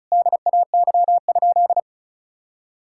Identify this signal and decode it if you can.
Morse code, heard as a keyed tone, about 36 wpm, 700 Hz. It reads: DAY?
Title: Sentences from Top 1000 Words - Review Part 6 - 35wpm